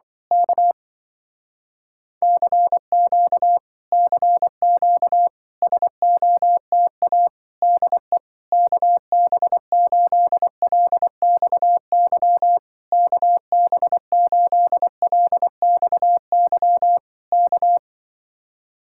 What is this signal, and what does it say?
Morse code, heard as a keyed tone, about 24 wpm, 700 Hz. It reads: K CQ CQ SOTA DE KB8LXY KB8LXY K
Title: Speed-Racing Common QSO Phrases Encore - 20wpm